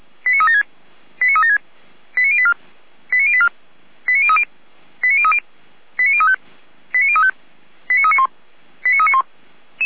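Five-tone selective call (ZVEI), heard as quick runs of five tones, tones of about 80 ms. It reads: 89376 89376 89084 89084 89030 89030 89035 89035 89391 89391
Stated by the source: ZVEI five-tone call received via SDR.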